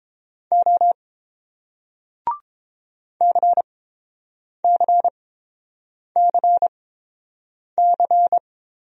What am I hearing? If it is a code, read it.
Morse code, heard as a keyed tone, about 30 wpm, spaced as for 7 wpm, 700 Hz. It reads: O CCCC